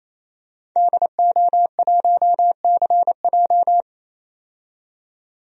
Morse code, heard as a keyed tone, about 28 wpm, 700 Hz. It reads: DO1CJ
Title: DXCC-100 Call Signs - 5 Characters - 28wpm